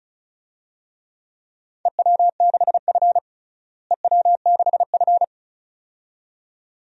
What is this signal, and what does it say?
Morse code, heard as a keyed tone, about 36 wpm, 700 Hz. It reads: EW6F EW6F
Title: DXCC-100 Call Signs - 4 Characters - Repeated 2x - 35wpm